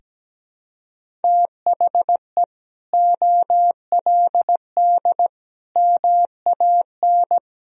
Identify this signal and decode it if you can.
Morse code, heard as a keyed tone, about 17 wpm, 700 Hz. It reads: THE OLD MAN
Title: Sentences from Top 1000 Words - Review Part 3 - 17wpm